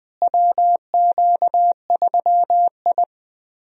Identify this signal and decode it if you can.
Morse code, heard as a keyed tone, about 20 wpm, 700 Hz. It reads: WQ3I
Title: USA Call Signs - 4 Characters - Repeated 2x - 20wpm